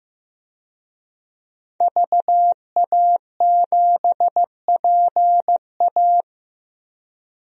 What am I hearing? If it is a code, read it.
Morse code, heard as a keyed tone, about 15 wpm, 700 Hz. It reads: VA7PA
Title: DXCC-100 Call Signs - 5 Characters - 15wpm